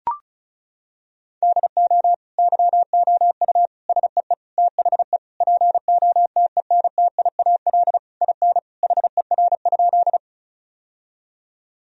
Morse code, heard as a keyed tone, about 34 wpm, 700 Hz. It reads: DO YOU SEE THE POTENTIAL IN HER?
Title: Sentences from Top 1000 Words - Review Part 4 - 35wpm